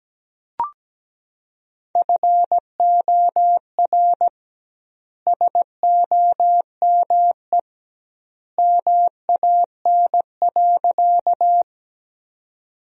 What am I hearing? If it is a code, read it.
Morse code, heard as a keyed tone, about 17 wpm, 700 Hz. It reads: FOR SOME MAN.
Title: 2x Word-Spacing - Sentences from Top 200 Words - 17wpm